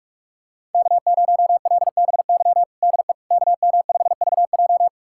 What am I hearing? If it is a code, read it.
Morse code, heard as a keyed tone, about 44 wpm, 700 Hz. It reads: K0LBY DE KM5VJ
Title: USA Callsign DE Callsign - 5 Characters - 45wpm